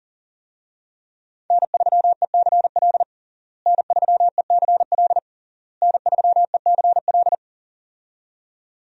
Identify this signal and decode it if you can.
Morse code, heard as a keyed tone, about 40 wpm, 700 Hz. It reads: N3ECL N3ECL N3ECL
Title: USA Call Signs - 5 Characters - Repeated 3x - 40wpm